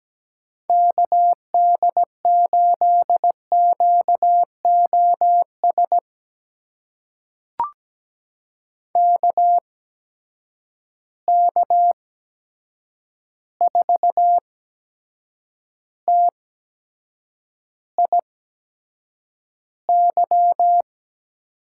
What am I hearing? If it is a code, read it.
Morse code, heard as a keyed tone, about 17 wpm, spaced as for 7 wpm, 700 Hz. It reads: KD8QOS KK4TIY
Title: USA Call Signs - 8x Character Spacing - 17wpm